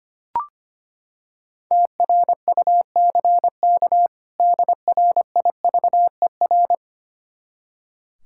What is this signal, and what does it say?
Morse code, heard as a keyed tone, about 25 wpm, 700 Hz. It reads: TRUCK DRIVER
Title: Practice QSO - Exchange 3 - More Info - 25wpm